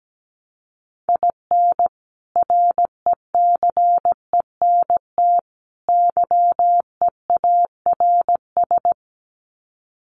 Morse code, heard as a keyed tone, about 17 wpm, 700 Hz. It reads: IN RECENT YEARS
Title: Sets of 3 Words 17wpm